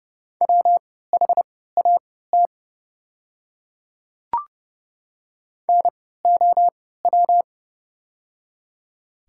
Morse code, heard as a keyed tone, about 30 wpm, spaced as for 17 wpm, 700 Hz. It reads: WHAT NOW
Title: 3x Character Spacing - Top 100 Words - 30wpm